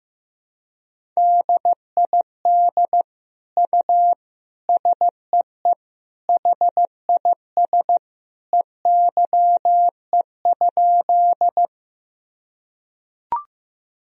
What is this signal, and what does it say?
Morse code, heard as a keyed tone, about 15 wpm, 700 Hz. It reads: DID U SEE HIS EYE?
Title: Sentences from Top 300 Words and Abbreviations 15wpm